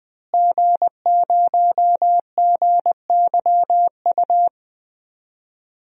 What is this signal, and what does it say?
Morse code, heard as a keyed tone, about 20 wpm, 700 Hz. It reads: G0GYU
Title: DXCC-100 Call Signs - 5 Characters - Repeated 3x - 20wpm